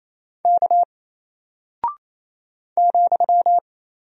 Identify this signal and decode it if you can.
Morse code, heard as a keyed tone, about 28 wpm, 700 Hz. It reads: K ,
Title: Single Character (Letter:Number:Punctuation:Prosign) Alt 28wpm